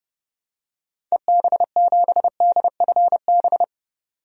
Morse code, heard as a keyed tone, about 32 wpm, 700 Hz. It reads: EB7DFB
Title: DXCC-100 Call Signs - 6 Characters - 30wpm